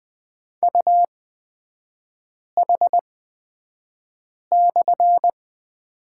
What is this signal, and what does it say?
Morse code, heard as a keyed tone, about 20 wpm, spaced as for 5 wpm, 700 Hz. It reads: UH/
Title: 243 Lesson 39e - Letter z - Three Characters 20wpm